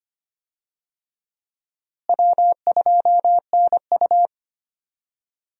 Morse code, heard as a keyed tone, about 25 wpm, 700 Hz. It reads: W2NU
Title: USA Call Signs - 4 Characters - Repeated 3x - 25wpm